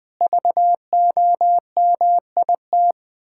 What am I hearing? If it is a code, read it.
Morse code, heard as a keyed tone, about 20 wpm, 700 Hz. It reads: VOMIT
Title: Pangram Challenge v2 - 20wpm